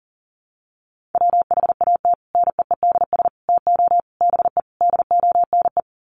Morse code, heard as a keyed tone, about 40 wpm, 700 Hz. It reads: WHAT NEEDS TO BE DONE
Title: Sets of 5 Words Encore 40wpm